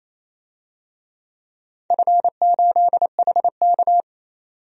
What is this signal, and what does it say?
Morse code, heard as a keyed tone, about 27 wpm, 700 Hz. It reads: F8HK
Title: DXCC-100 Call Signs - 4 Characters - 28wpm